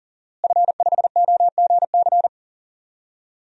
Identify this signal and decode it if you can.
Morse code, heard as a keyed tone, about 40 wpm, 700 Hz. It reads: F5OGC
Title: DXCC-100 Call Signs - 5 Characters - Repeated 2x - 40wpm